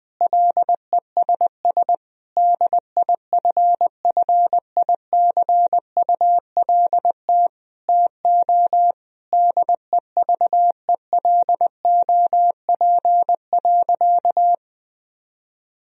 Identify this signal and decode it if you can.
Morse code, heard as a keyed tone, about 20 wpm, 700 Hz. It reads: LESS DIFFICULT TO DEVELOP.
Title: Sentences from Top 1000 Words - Review Part 2 - 20wpm